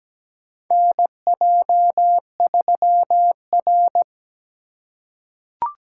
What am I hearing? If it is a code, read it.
Morse code, heard as a keyed tone, about 17 wpm, 700 Hz. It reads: NJ3R
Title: USA Call Signs - 4 Characters - 4x Character Spacing - 17wpm